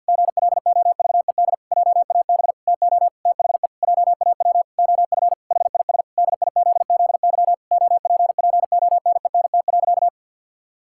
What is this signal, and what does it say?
Morse code, heard as a keyed tone, about 52 wpm, 700 Hz. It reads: GLOVED JAB TO THE JAW OF HIS DIZZY OPPONENT.
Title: Pangram Challenge v2 - 50wpm